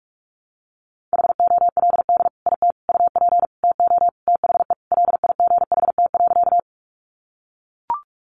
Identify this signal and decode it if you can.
Morse code, heard as a keyed tone, about 44 wpm, 700 Hz. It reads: HOLD IT UP TO THE LIGHT.
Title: Sentences from Top 400 Words 45wpm